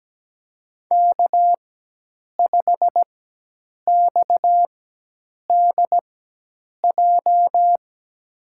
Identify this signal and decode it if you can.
Morse code, heard as a keyed tone, about 17 wpm, spaced as for 8 wpm, 700 Hz. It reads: K5XDJ
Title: USA Call Signs - 5 Characters - 4x Character Spacing - 17wpm